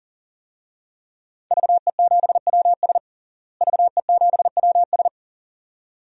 Morse code, heard as a keyed tone, about 40 wpm, 700 Hz. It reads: VE7WS VE7WS
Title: DXCC-100 Call Signs - 5 Characters - Repeated 2x - 40wpm